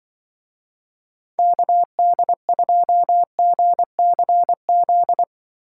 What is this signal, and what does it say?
Morse code, heard as a keyed tone, about 25 wpm, 700 Hz. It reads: KD2GCZ
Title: Speed Racing - USA 2x3 Call Signs 20wpm